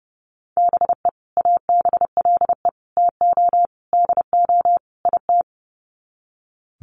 Morse code, heard as a keyed tone, about 30 wpm, 700 Hz. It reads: BE ABLE TO DO IT
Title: Sets of 5 Words Encore2 30wpm